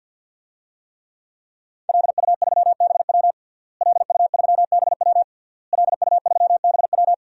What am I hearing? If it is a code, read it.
Morse code, heard as a keyed tone, about 52 wpm, 700 Hz. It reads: LU3BW LU3BW LU3BW
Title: DXCC-100 Call Signs - 5 Characters - Repeated 3x - 50wpm